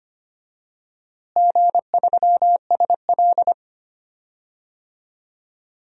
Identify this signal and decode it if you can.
Morse code, heard as a keyed tone, about 25 wpm, 700 Hz. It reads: G3SL
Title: DXCC-100 Call Signs - 4 Characters - Repeated 2x - 25wpm